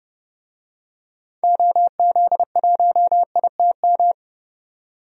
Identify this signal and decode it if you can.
Morse code, heard as a keyed tone, about 30 wpm, 700 Hz. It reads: OZ1ITM
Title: DXCC-100 Call Signs - 6 Characters - 30wpm